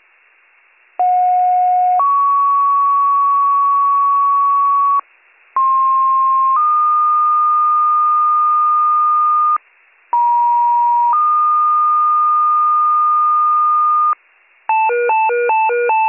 Two-tone page, heard as a one-second tone, then a three-second tone, about 730 then 1120 Hz, about 1060 then 1220 Hz, about 950 then 1220 Hz. It is heard from 1.0 to 14.1 s.